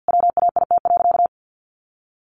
Morse code, heard as a keyed tone, about 50 wpm, 700 Hz. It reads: WAIT.
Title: Sentences from Top 800 Words 50wpm